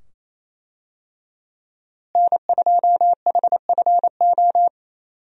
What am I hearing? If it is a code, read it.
Morse code, heard as a keyed tone, about 28 wpm, 700 Hz. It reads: N2HFO
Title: USA Call Signs - 5 Characters - Repeated 2x - 28wpm